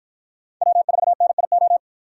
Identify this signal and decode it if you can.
Morse code, heard as a keyed tone, about 50 wpm, 700 Hz. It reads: W4NIO